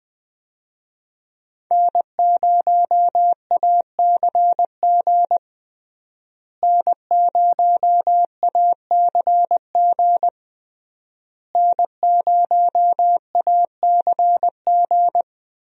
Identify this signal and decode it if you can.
Morse code, heard as a keyed tone, about 20 wpm, 700 Hz. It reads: N0ACG N0ACG N0ACG
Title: USA Call Signs - 5 Characters - Repeated 3x - 20wpm